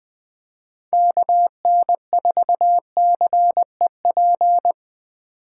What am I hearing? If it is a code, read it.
Morse code, heard as a keyed tone, about 24 wpm, 700 Hz. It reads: KN4CEP
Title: Speed Racing - USA Call Signs 20wpm